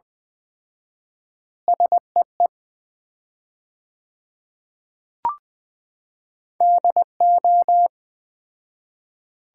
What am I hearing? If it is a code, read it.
Morse code, heard as a keyed tone, about 20 wpm, 700 Hz. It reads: SEE DO
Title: 4x Word-Spacing - Sets of 4 Words - 20wpm